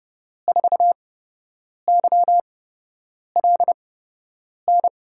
Morse code, heard as a keyed tone, about 30 wpm, spaced as for 8 wpm, 700 Hz. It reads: 4YLN
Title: USA Call Signs - 6 Characters - 8x Character Spacing - 30wpm